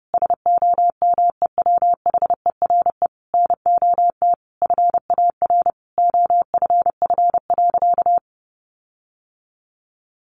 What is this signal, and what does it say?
Morse code, heard as a keyed tone, about 30 wpm, 700 Hz. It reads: SOMEWHERE NOT FAR OFF.
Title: Sentences from Top 1400 Words 30wpm